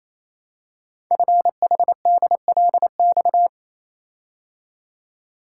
Morse code, heard as a keyed tone, about 27 wpm, 700 Hz. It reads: FHDLX
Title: Code Groups - 5 Letter:Numbers - 28wpm